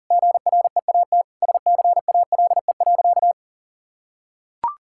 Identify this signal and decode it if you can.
Morse code, heard as a keyed tone, about 40 wpm, 700 Hz. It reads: GREAT SCALE.
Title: Sentences from Top 1000 Words - Review Part 5 - 40wpm